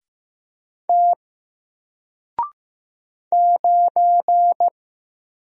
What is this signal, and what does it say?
Morse code, heard as a keyed tone, about 15 wpm, 700 Hz. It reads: T 9
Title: Single Letter-Number Alt 15wpm